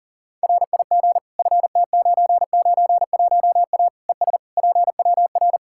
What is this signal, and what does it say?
Morse code, heard as a keyed tone, about 40 wpm, 700 Hz. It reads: RIG FT991A ES PWR
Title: K1USN SST QSO Practice 40wpm